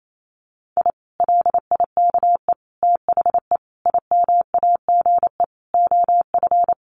Morse code, heard as a keyed tone, about 28 wpm, 700 Hz. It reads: I LIKE THE IMAGE OF
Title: Sentences from Top 700 Words 28wpm